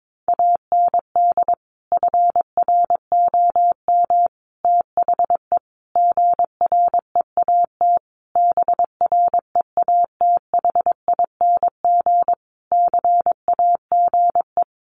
Morse code, heard as a keyed tone, about 22 wpm, 700 Hz. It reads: AND FROM THE GREAT BREATHING CAGE
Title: A Sound Of Thunder 22wpm